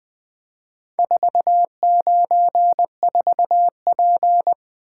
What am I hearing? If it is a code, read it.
Morse code, heard as a keyed tone, about 20 wpm, 700 Hz. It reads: 494P
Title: Four Letter-Number Combinations - 20wpm